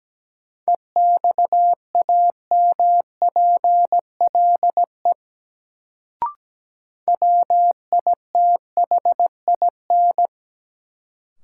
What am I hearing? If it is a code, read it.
Morse code, heard as a keyed tone, about 17 wpm, 700 Hz. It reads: EXAMPLE WITHIN
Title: Sets of 2 Words - New Word Drill - 17wpm